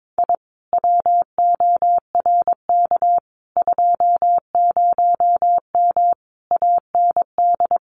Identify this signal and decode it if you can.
Morse code, heard as a keyed tone, about 22 wpm, 700 Hz. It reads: I WORK 20M AND
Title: Common QSO Phrases Encore 22wpm